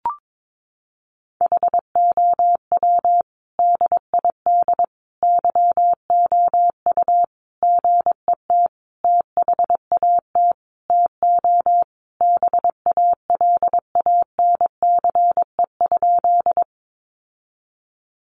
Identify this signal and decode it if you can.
Morse code, heard as a keyed tone, about 22 wpm, 700 Hz. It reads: HOW DID YOU GET THAT TO BALANCE?
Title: Sentences from Top 900 Words 22wpm